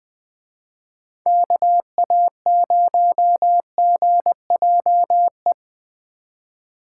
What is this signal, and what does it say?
Morse code, heard as a keyed tone, about 20 wpm, 700 Hz. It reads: KA0GJE
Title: Speed Racing - USA Call Signs 17wpm